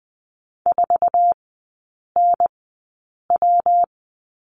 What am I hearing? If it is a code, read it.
Morse code, heard as a keyed tone, about 20 wpm, spaced as for 8 wpm, 700 Hz. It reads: 4NW
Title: 138 Lesson 23e - Letter f - Three Characters 20wpm